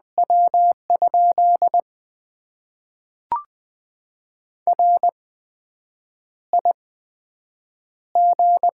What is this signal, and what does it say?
Morse code, heard as a keyed tone, about 20 wpm, 700 Hz. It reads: W ? R I G
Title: Top 20 QSO Elements - 8x Character Spacing - 20wpm